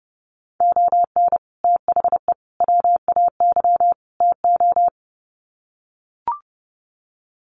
Morse code, heard as a keyed tone, about 30 wpm, 700 Hz. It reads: ON THE WAY TO
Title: Sets of 4 Words 30wpm